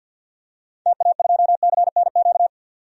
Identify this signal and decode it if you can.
Morse code, heard as a keyed tone, about 47 wpm, 700 Hz. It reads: TA2CNQ
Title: DXCC-100 Call Signs - 6 Characters - 50wpm